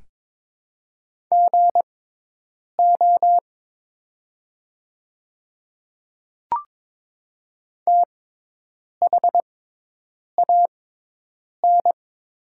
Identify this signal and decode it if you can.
Morse code, heard as a keyed tone, about 22 wpm, spaced as for 8 wpm, 700 Hz. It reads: GO THAN
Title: 6x Character Spacing - Top 200 Words - 22wpm